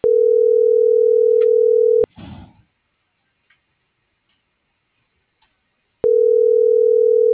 Ringback tone (North America), 440 + 480 Hz.